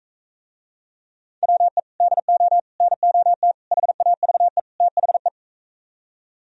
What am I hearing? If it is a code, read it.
Morse code, heard as a keyed tone, about 44 wpm, 700 Hz. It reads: WE DO NOT HAVE THE